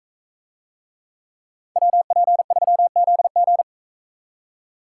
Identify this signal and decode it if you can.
Morse code, heard as a keyed tone, about 41 wpm, 700 Hz. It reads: WP3ZG